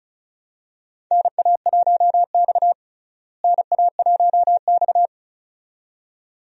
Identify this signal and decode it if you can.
Morse code, heard as a keyed tone, about 36 wpm, 700 Hz. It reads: NA1X NA1X
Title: USA Call Signs - 4 Characters - Repeated 2x - 35wpm